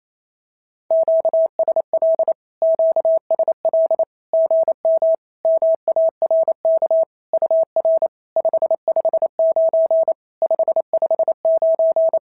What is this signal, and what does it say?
Morse code, heard as a keyed tone, about 28 wpm, 690 Hz. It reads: QSL QSL GM MARK UR 559 559
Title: SOTA QSO Practice 28wpm